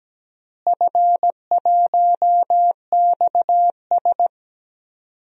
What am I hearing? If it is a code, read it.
Morse code, heard as a keyed tone, about 17 wpm, 700 Hz. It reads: F1XS
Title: DXCC-100 Call Signs - 4 Characters - Repeated 2x - 17wpm